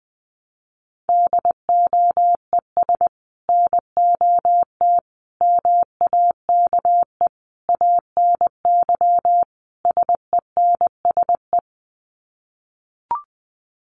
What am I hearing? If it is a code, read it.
Morse code, heard as a keyed tone, about 20 wpm, 700 Hz. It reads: DOES NOT MAKE ANY SENSE
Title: Sets of 5 Words Encore2 20wpm